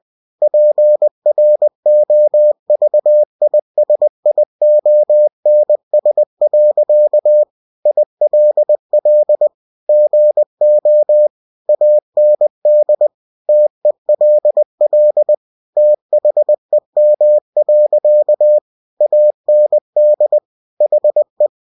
Morse code, heard as a keyed tone, about 20 wpm, 600 Hz. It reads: PROVISIONS. ILL GO AND TELL THEM. AND HE
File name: cw-books / winnie-the-pooh / winnie-20wpm-page-0077